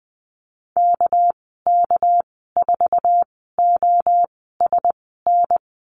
Morse code, heard as a keyed tone, about 20 wpm, spaced as for 14 wpm, 700 Hz. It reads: KK4OSN
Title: 231 Lesson 37g - Letter x - Call Signs 20wpm